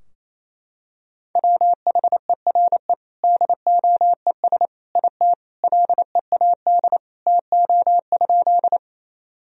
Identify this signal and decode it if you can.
Morse code, heard as a keyed tone, about 27 wpm, 700 Hz. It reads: WHERE DOES IT LEAD TO?
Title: Sentences from Top 1000 Words - Review Part 2 - 28wpm